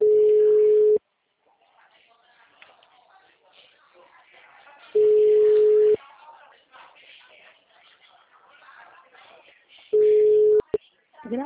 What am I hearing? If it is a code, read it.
Ringback tone (Europe, 425 Hz), with speech.
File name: out- -2014- - - .18073